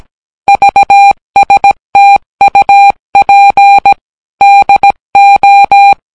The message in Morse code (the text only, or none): VSTUP DO